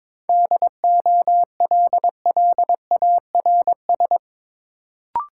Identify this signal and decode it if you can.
Morse code, heard as a keyed tone, about 22 wpm, 700 Hz. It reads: DOLLARS
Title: Sets of 5 Words - New Word Drill - 22wpm